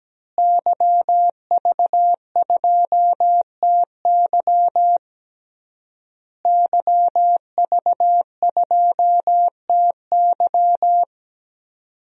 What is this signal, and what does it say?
Morse code, heard as a keyed tone, about 17 wpm, 700 Hz. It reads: YV2TY YV2TY
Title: DXCC-100 Call Signs - 5 Characters - Repeated 3x - 17wpm